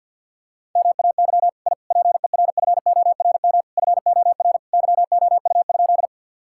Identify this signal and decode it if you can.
Morse code, heard as a keyed tone, about 50 wpm, 700 Hz. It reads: MAY I PERFORM FOR YOU?
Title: Sentences from Top 1000 Words - Review Part 5 - 50wpm